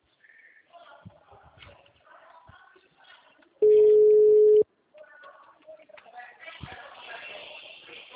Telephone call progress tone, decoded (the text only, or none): ringback tone